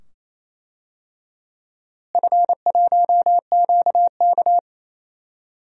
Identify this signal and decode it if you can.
Morse code, heard as a keyed tone, about 28 wpm, 700 Hz. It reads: F1QK
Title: DXCC-100 Call Signs - 4 Characters - Repeated 2x - 28wpm